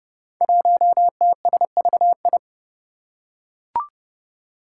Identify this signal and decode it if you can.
Morse code, heard as a keyed tone, about 29 wpm, 700 Hz. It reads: 1TSVI
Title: Five Challenging Letter-Number Combinations 30wpm